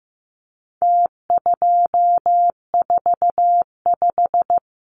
Morse code, heard as a keyed tone, about 15 wpm, 700 Hz. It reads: T245
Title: Numbers - Contest Style - 4-Digit 15wpm